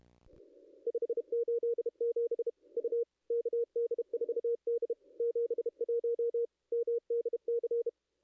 Morse code, heard as a keyed tone, about 32 wpm, 440 Hz. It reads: 587 U KD4D 71 MDC
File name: contests / ss-cw-k5tr-op-n5rz -0400